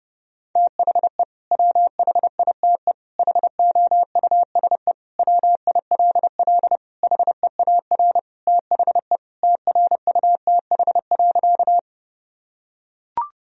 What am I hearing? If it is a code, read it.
Morse code, heard as a keyed tone, about 32 wpm, 700 Hz. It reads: THE WHITE HOUSE WILL HEAR THE TRUTH.
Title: Sentences from Top 1000 Words - Review Part 1 - 30wpm